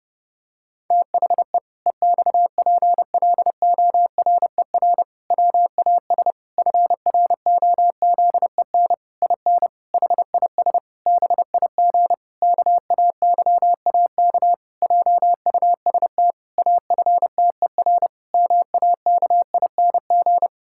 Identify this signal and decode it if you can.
Morse code, heard as a keyed tone, about 30 wpm, 700 Hz. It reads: THE EXPLORER WAS FROZEN IN HIS BIG KAYAK JUST AFTER MAKING
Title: Pangram Challenge v2 - 30wpm